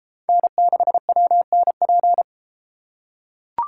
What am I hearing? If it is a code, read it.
Morse code, heard as a keyed tone, about 30 wpm, 700 Hz. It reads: N6WNP